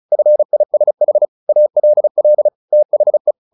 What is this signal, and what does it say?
Morse code, heard as a keyed tone, about 34 wpm, 600 Hz. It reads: FISH ALL THE
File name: the old man and the sea 2 35wpm